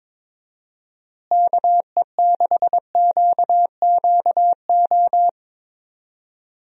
Morse code, heard as a keyed tone, about 22 wpm, 700 Hz. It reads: KE6QQO
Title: USA Call Signs - 6 Characters - Repeated 3x - 22wpm